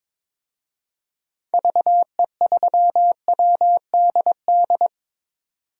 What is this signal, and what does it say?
Morse code, heard as a keyed tone, about 22 wpm, 700 Hz. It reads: VE3WDD
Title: DXCC-100 Call Signs - 6 Characters - 22wpm